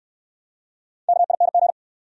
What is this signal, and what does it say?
Morse code, heard as a keyed tone, about 69 wpm, 690 Hz. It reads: KING